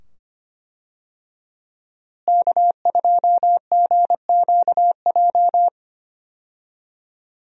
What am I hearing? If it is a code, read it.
Morse code, heard as a keyed tone, about 25 wpm, 700 Hz. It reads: K2GQJ
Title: USA Call Signs - 5 Characters - Repeated 2x - 25wpm